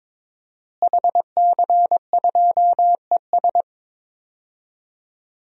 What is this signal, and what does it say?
Morse code, heard as a keyed tone, about 22 wpm, 700 Hz. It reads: HC2ES
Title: DXCC-100 Call Signs - 5 Characters - Repeated 3x - 22wpm